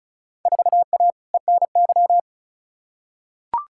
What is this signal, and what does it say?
Morse code, heard as a keyed tone, about 36 wpm, 700 Hz. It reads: 4A ENY
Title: Field Day Exchanges - 35wpm